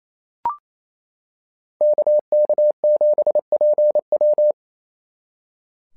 Morse code, heard as a keyed tone, about 28 wpm, 600 Hz. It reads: KK7PW
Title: SOTA QSO Practice 28wpm